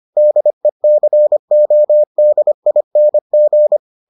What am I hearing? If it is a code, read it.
Morse code, heard as a keyed tone, about 25 wpm, 600 Hz. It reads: DECODING